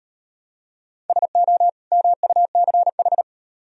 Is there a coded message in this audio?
Yes: Morse code from 1.1 to 3.2 s, heard as a keyed tone, about 38 wpm, 700 Hz. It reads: SO MUCH